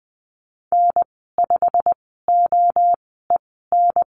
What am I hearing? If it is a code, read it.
Morse code, heard as a keyed tone, about 20 wpm, spaced as for 14 wpm, 700 Hz. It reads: N5OEN
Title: 088 Lesson 15g - Number 5 - Call Signs 20wpm